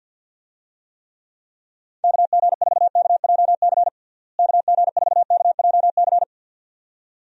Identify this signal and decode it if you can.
Morse code, heard as a keyed tone, about 50 wpm, 700 Hz. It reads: KG4KJC KG4KJC
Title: USA Call Signs - 6 Characters - Repeated 2x - 50wpm